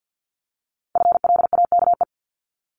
Morse code, heard as a keyed tone, about 50 wpm, 700 Hz. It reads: FLAKE
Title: Multi-letter Phonemes Repeated 3x 50wpm